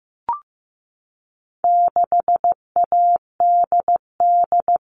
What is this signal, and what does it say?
Morse code, heard as a keyed tone, about 15 wpm, 700 Hz. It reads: 6ADD
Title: All Cut Numbers - 4-Digit 15wpm